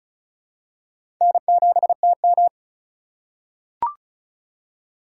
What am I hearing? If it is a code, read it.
Morse code, heard as a keyed tone, about 36 wpm, 690 Hz. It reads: N7TM
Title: USA Call Signs - 4 Characters - 2x Character Spacing - 35wpm